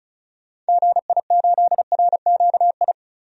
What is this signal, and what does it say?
Morse code, heard as a keyed tone, about 34 wpm, 700 Hz. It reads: GI8RQI
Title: DXCC-100 Call Signs - 6 Characters - Repeated 2x - 35wpm